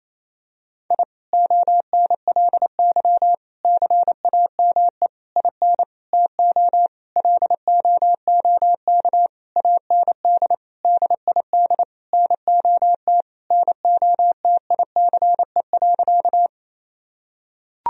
Morse code, heard as a keyed tone, about 27 wpm, 700 Hz. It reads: I ONLY CAME IN TO LOOK AND DID NOT NOTICE.
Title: Sentences from Top 1000 Words - Review Part 2 - 28wpm